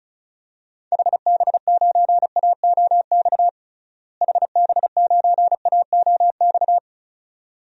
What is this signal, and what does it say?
Morse code, heard as a keyed tone, about 36 wpm, 700 Hz. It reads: HB9AOX HB9AOX
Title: DXCC-100 Call Signs - 6 Characters - Repeated 2x - 35wpm